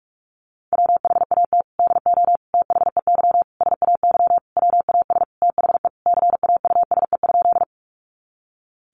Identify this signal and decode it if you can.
Morse code, heard as a keyed tone, about 44 wpm, 700 Hz. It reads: WHAT DO THEY SAY WAS THE CAUSE?
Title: Sentences from Top 200 Words 45wpm